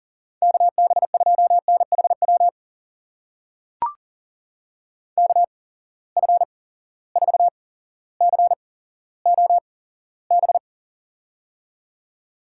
Morse code, heard as a keyed tone, about 40 wpm, spaced as for 17 wpm, 700 Hz. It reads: KB2NHW KF4COB
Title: USA Call Signs - 6 Characters - 8x Character Spacing - 40wpm